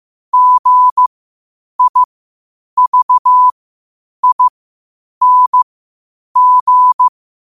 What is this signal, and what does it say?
Morse code, heard as a keyed tone, about 15 wpm, spaced as for 8 wpm, 1000 Hz. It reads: GIVING
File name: qotdfiles / 8wpm